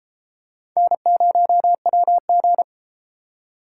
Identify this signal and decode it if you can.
Morse code, heard as a keyed tone, about 32 wpm, 700 Hz. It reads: N0WG